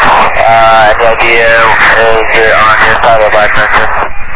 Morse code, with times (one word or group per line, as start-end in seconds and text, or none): none